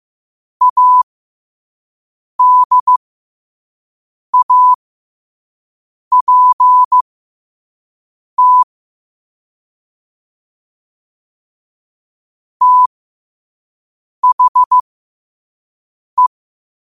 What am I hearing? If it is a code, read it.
Morse code, heard as a keyed tone, about 15 wpm, spaced as for 5 wpm, 1000 Hz. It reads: ADAPT THE
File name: qotdfiles / 5wpm